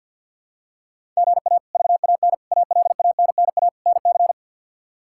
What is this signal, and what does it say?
Morse code, heard as a keyed tone, about 49 wpm, 700 Hz. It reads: GA VAN ALANNA NC
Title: K1USN SST QSO Practice - 50wpm